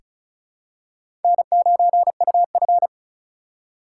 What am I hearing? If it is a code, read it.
Morse code, heard as a keyed tone, about 34 wpm, 700 Hz. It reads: N9UF
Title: USA Call Signs - 4 Characters - 4x Character Spacing - 35wpm